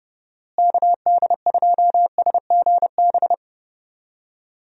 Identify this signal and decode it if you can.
Morse code, heard as a keyed tone, about 30 wpm, 700 Hz. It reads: KD2SGB
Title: USA Call Signs - 6 Characters - Repeated 2x - 30wpm